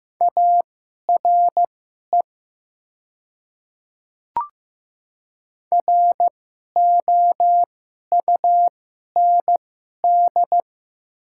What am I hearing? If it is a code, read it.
Morse code, heard as a keyed tone, about 15 wpm, spaced as for 11 wpm, 700 Hz. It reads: ARE ROUND
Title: 2x Character Spacing - Top 500 Words - 15wpm